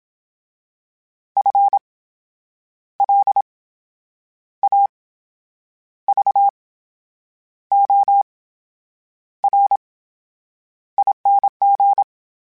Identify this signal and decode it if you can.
Morse code, heard as a keyed tone, about 27 wpm, spaced as for 8 wpm, 800 Hz. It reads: FLAVORING
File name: Special INGs / Special INGs-mp3s / VerySlowToFaster Ing @25